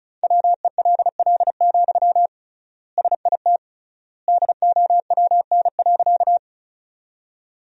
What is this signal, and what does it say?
Morse code, heard as a keyed tone, about 36 wpm, 700 Hz. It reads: WELL, SIT DOWN.
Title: 3x Word-Spacing - Sentences from Top 500 Words - 35wpm